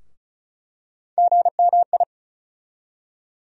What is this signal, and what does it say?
Morse code, heard as a keyed tone, about 34 wpm, 700 Hz. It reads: GMI
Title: USA Call Sign Postfixes - 3 Letters 35wpm